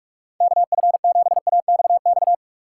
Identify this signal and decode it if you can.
Morse code, heard as a keyed tone, about 46 wpm, 700 Hz. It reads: KF7AXX
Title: USA Call Signs - 6 Characters - 4x Character Spacing - 45wpm